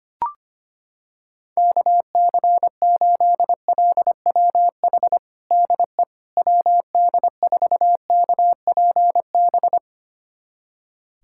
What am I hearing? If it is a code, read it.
Morse code, heard as a keyed tone, about 25 wpm, 700 Hz. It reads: KC8LWH DE WD4KPB
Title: USA Callsign DE Callsign 25wpm